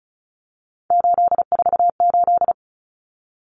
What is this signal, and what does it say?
Morse code, heard as a keyed tone, about 34 wpm, 700 Hz. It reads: 848
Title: Common Cut Numbers - 3-Digit 35wpm